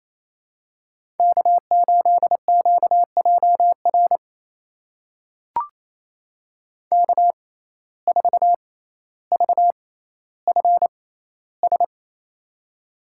Morse code, heard as a keyed tone, about 28 wpm, spaced as for 14 wpm, 700 Hz. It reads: K8QJR K4VFS
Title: USA Call Signs - 5 Characters - 6x Character Spacing - 28wpm